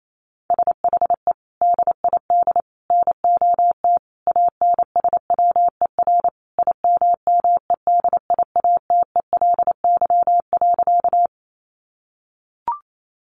Morse code, heard as a keyed tone, about 28 wpm, 700 Hz. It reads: SHE DID NOT ANSWER IMMEDIATELY.
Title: Sentences from Top 1100 Words 28wpm